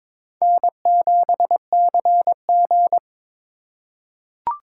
Morse code, heard as a keyed tone, about 22 wpm, 700 Hz. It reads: N7CG
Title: USA Call Signs - 4 Characters - 4x Character Spacing - 22wpm